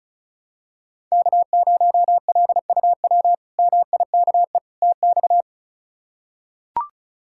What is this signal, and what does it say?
Morse code, heard as a keyed tone, about 34 wpm, 700 Hz. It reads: K0LUW MIKE TX
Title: K1USN SST QSO Practice - 35wpm